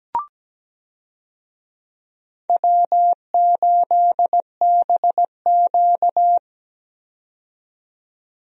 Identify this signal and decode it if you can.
Morse code, heard as a keyed tone, about 17 wpm, 700 Hz. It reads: W8BQ
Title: USA Call Signs - 4 Characters - 17wpm